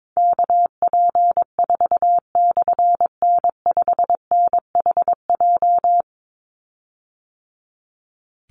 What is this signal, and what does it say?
Morse code, heard as a keyed tone, about 22 wpm, 700 Hz. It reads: KP4/N5NHJ
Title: Call Signs - 22wpm